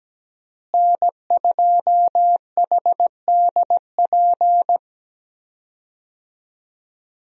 Morse code, heard as a keyed tone, about 17 wpm, 700 Hz. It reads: N2HDP
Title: USA Call Signs - 5 Characters - Repeated 2x - 17wpm